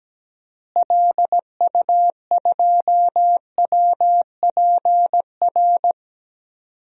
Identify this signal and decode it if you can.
Morse code, heard as a keyed tone, about 17 wpm, 700 Hz. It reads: LU2WPR
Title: DXCC-100 Call Signs - 6 Characters - Repeated 2x - 17wpm